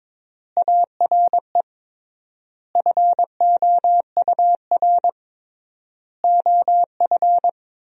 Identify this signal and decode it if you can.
Morse code, heard as a keyed tone, about 22 wpm, 700 Hz. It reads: ARE FOUR OF
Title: 3x Word-Spacing - Sentences from Top 300 Words - 22wpm